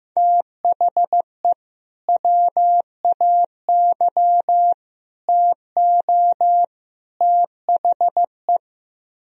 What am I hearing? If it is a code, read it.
Morse code, heard as a keyed tone, about 15 wpm, 700 Hz. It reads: THE WAY TO THE
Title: Speed-Racing - Sets of 4 Words 15wpm